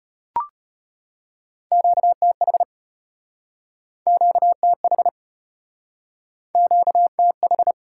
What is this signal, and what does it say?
Morse code, heard as a keyed tone, about 34 wpm, 700 Hz. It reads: QTH QTH QTH